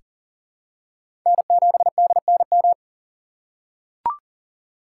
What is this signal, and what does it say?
Morse code, heard as a keyed tone, about 40 wpm, 700 Hz. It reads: N7DNM
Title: USA Call Signs - 5 Characters - 3x Character Spacing - 40wpm